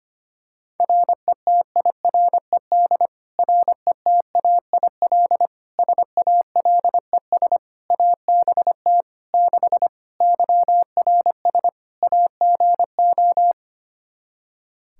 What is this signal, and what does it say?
Morse code, heard as a keyed tone, about 25 wpm, 700 Hz. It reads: RETIRED RETAIL SALES ABT 6 YRS AGO
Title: Practice QSO - Exchange 3 - More Info - 25wpm